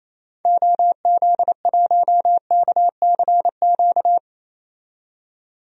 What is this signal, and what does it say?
Morse code, heard as a keyed tone, about 28 wpm, 700 Hz. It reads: OZ1KCQ
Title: DXCC-100 Call Signs - 6 Characters - 28wpm